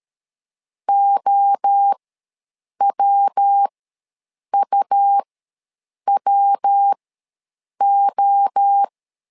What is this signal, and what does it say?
Morse code, heard as a keyed tone, about 13 wpm, spaced as for 7 wpm, 800 Hz. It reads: OWUWO
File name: LEZIONI / 3 MWRO / 3 MWRO ws 12wpm 1914